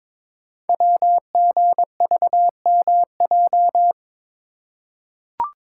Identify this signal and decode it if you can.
Morse code, heard as a keyed tone, about 22 wpm, 700 Hz. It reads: WGVMJ
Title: Code Groups - 5 Letters - 22wpm